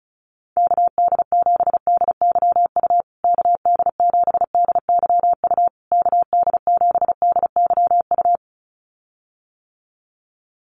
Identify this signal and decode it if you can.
Morse code, heard as a keyed tone, about 34 wpm, 700 Hz. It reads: KD7DYU KD7DYU KD7DYU
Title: USA Call Signs Repeated 3x 35wpm